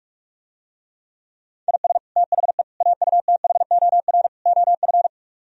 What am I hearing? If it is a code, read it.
Morse code, heard as a keyed tone, about 49 wpm, 700 Hz. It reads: IS THE AUTHOR OF